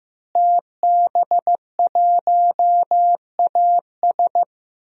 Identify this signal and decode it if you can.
Morse code, heard as a keyed tone, about 15 wpm, 700 Hz. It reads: TB1AS
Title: DXCC-100 Call Signs - 5 Characters - Repeated 2x - 15wpm